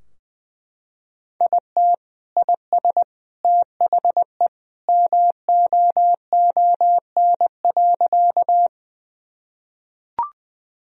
Morse code, heard as a keyed tone, about 20 wpm, 700 Hz. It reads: IT IS THE MOON.
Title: Sentences from Top 1000 Words - Review Part 3 - 20wpm